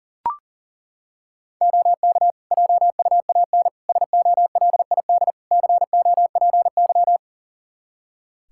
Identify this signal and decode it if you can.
Morse code, heard as a keyed tone, about 40 wpm, 700 Hz. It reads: OK JUAN SOLID COPY
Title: K1USN SST QSO Practice 40wpm